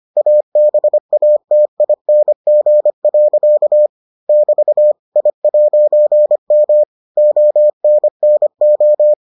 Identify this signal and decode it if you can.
Morse code, heard as a keyed tone, about 25 wpm, 600 Hz. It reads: ABATING. <BT> I'M ONNO